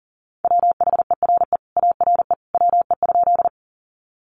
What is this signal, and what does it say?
Morse code, heard as a keyed tone, about 40 wpm, 700 Hz. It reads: WHERE ARE WE?
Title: Sentences from Top 200 Words 40wpm